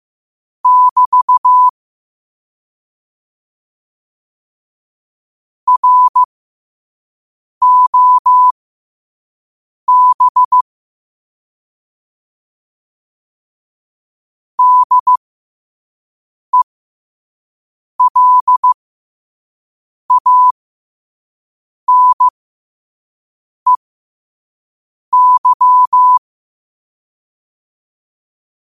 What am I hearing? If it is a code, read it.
Morse code, heard as a keyed tone, about 15 wpm, spaced as for 5 wpm, 1000 Hz. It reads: <BT> ROB DELANEY
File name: qotdfiles / 5wpm